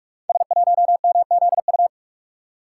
Morse code, heard as a keyed tone, about 44 wpm, 700 Hz. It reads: S1MZU
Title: Code Groups - 5 Letter:Numbers - 45wpm